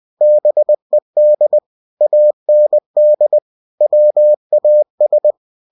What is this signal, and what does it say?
Morse code, heard as a keyed tone, about 20 wpm, 600 Hz. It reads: BED AND WAS
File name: cw-books / winnie-the-pooh / winnie-20wpm-page-0091